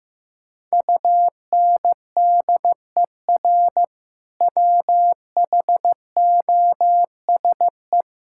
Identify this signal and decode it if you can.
Morse code, heard as a keyed tone, about 15 wpm, 700 Hz. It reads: UNDER WHOSE
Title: Sentences from Top 1600 Words 15wpm